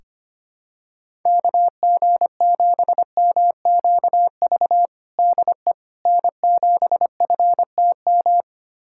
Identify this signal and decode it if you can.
Morse code, heard as a keyed tone, about 25 wpm, 700 Hz. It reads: KG7MQV DE N7FTM
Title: USA Callsign DE Callsign 25wpm